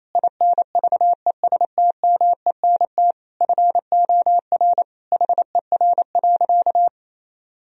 Morse code, heard as a keyed tone, about 28 wpm, 700 Hz. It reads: INVESTMENT FOR HER.
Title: Sentences from Top 1000 Words - Review Part 4 - 28wpm